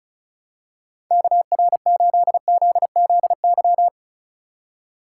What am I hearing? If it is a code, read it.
Morse code, heard as a keyed tone, about 36 wpm, 700 Hz. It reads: KR8ZZY
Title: USA Call Signs - 6 Characters - 2x Character Spacing - 35wpm